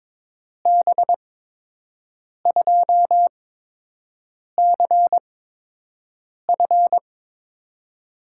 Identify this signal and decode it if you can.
Morse code, heard as a keyed tone, about 22 wpm, spaced as for 6 wpm, 700 Hz. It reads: B2CF
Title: USA Call Signs - 5 Characters - 8x Character Spacing - 22wpm